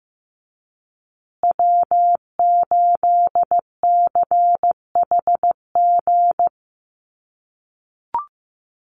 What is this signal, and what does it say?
Morse code, heard as a keyed tone, about 15 wpm, 700 Hz. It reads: W8CHG
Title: USA Call Signs Repeated 3x 15wpm